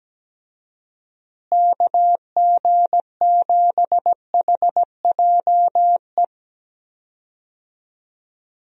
Morse code, heard as a keyed tone, about 17 wpm, 700 Hz. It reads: KG7HJE
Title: USA Call Signs - 6 Characters - Repeated 3x - 17wpm